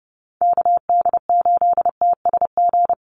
Morse code, heard as a keyed tone, about 29 wpm, 700 Hz. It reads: KD8TSG
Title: USA Call Signs 30wpm